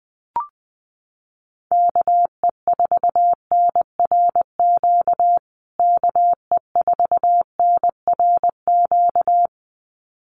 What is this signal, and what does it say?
Morse code, heard as a keyed tone, about 20 wpm, 700 Hz. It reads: KE4NRQ KE4NRQ
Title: USA Call Signs Repeated 2x 20wpm